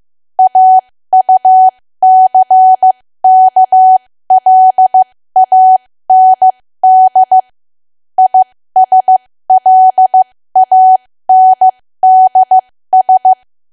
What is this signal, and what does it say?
Morse code, heard as a keyed tone, about 15 wpm, spaced as for 13 wpm, 750 Hz. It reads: AUCKLAND ISLANDS
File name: cwa / W1AW-Level2 / 13WPM